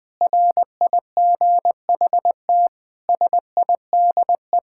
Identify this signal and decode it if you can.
Morse code, heard as a keyed tone, about 20 wpm, 700 Hz. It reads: RIGHT SIDE
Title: Sentences with 6-Word Phrases - 20wpm